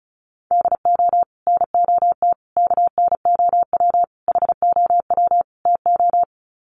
Morse code, heard as a keyed tone, about 34 wpm, 700 Hz. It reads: DO NOT KNOW HOW TO
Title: Sets of 5 Words 35wpm